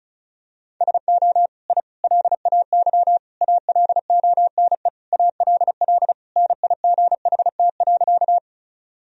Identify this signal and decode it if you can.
Morse code, heard as a keyed tone, about 36 wpm, 700 Hz. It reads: SO I LAY ALONE ALL NIGHT.
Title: Sentences from Top 1000 Words - Review Part 2 - 35wpm